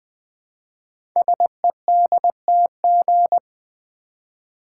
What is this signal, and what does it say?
Morse code, heard as a keyed tone, about 20 wpm, 700 Hz. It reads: SEDTG
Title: Code Groups - 5 Letter:Numbers - 20wpm